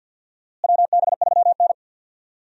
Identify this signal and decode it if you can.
Morse code, heard as a keyed tone, about 47 wpm, 700 Hz. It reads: WB3N
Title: USA Call Signs - 4 Characters - Repeated 2x - 50wpm